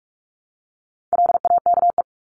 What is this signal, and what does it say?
Morse code, heard as a keyed tone, about 44 wpm, 700 Hz. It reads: LAKE
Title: Top 1001-1250 Words 45wpm